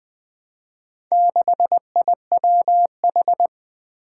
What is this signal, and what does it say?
Morse code, heard as a keyed tone, about 21 wpm, 700 Hz. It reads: 6IWH
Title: Four Challenging Letter-Number Combinations 20wpm